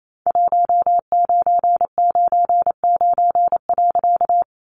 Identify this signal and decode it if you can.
Morse code, heard as a keyed tone, about 28 wpm, 700 Hz. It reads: 1999.
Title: A Sound Of Thunder 28wpm